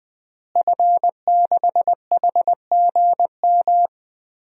Morse code, heard as a keyed tone, about 20 wpm, 700 Hz. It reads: F6HGM
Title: DXCC-100 Call Signs - 5 Characters - Repeated 2x - 20wpm